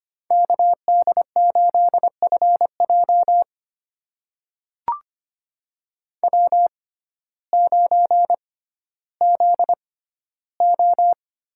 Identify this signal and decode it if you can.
Morse code, heard as a keyed tone, about 25 wpm, spaced as for 14 wpm, 700 Hz. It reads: KD8FJ W9ZO
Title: USA Call Signs - 5 Characters - 6x Character Spacing - 25wpm